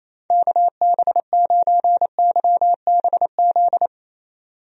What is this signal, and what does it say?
Morse code, heard as a keyed tone, about 28 wpm, 700 Hz. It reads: KB9YBZ
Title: USA Call Signs - 6 Characters - Repeated 3x - 28wpm